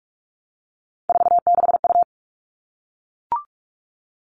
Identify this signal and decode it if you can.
Morse code, heard as a keyed tone, about 44 wpm, 700 Hz. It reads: 46U
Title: All Cut Numbers - 3-Digit 45wpm